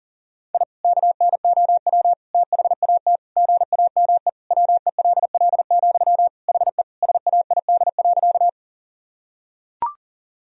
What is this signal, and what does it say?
Morse code, heard as a keyed tone, about 40 wpm, 700 Hz. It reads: I KNOW THAT GAME WELL, HE SAID.
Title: Sentences from Top 1000 Words - Review Part 3 - 40wpm